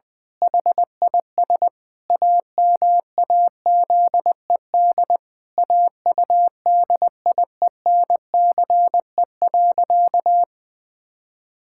Morse code, heard as a keyed tone, about 20 wpm, 700 Hz. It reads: HIS AMAZED AUDIENCE.
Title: Pangram Challenge v2 - 20wpm